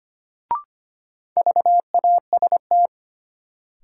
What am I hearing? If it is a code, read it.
Morse code, heard as a keyed tone, about 25 wpm, 700 Hz. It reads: VAST
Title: Sentences from Top 1300 Words - New Word Drill - 25wpm